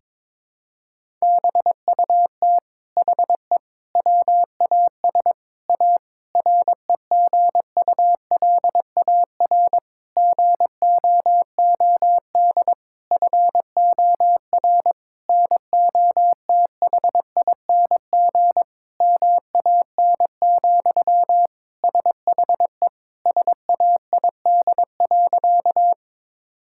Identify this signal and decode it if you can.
Morse code, heard as a keyed tone, about 22 wpm, 700 Hz. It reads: BUT HE WAS A REGULAR GOOD FOR NOTHING MAN, SHE SAID.
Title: Sentences from Top 1000 Words - Review Part 6 - 22wpm